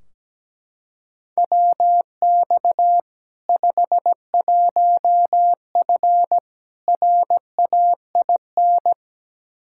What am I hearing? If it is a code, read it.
Morse code, heard as a keyed tone, about 17 wpm, 700 Hz. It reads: WX 51F RAIN
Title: K1USN SST QSO Practice 17wpm